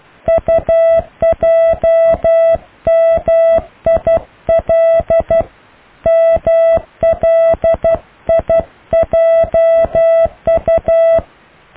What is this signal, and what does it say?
Morse code, heard as a keyed tone, about 12 wpm, 660 Hz. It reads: UJMIL MLIJU
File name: cw / IJLMU